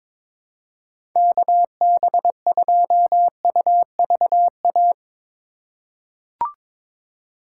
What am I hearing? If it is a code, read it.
Morse code, heard as a keyed tone, about 22 wpm, 700 Hz. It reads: KB2UVA
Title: USA Call Signs - 2x Character Spacing - 22wpm